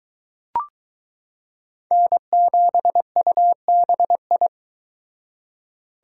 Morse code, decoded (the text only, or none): N7UBI